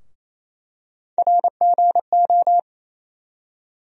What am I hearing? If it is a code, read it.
Morse code, heard as a keyed tone, about 28 wpm, 700 Hz. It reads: RGO
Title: USA Call Sign Postfixes - 3 Letters 28wpm